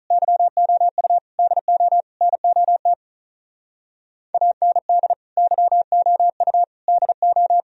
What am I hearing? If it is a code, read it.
Morse code, heard as a keyed tone, about 43 wpm, 700 Hz. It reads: YOU DO NOT AND YOU DO
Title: Speed-Racing - Sets of 4 Words 35wpm